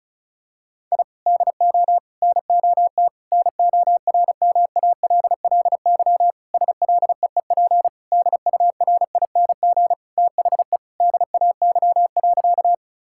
Morse code, heard as a keyed tone, about 36 wpm, 700 Hz. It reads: I DO NOT NORMALLY SLEEP DURING THE DAY.
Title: Sentences from Top 1000 Words - Review Part 6 - 35wpm